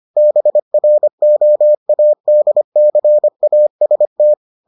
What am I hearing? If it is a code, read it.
Morse code, heard as a keyed tone, about 25 wpm, 600 Hz. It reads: BROADCAST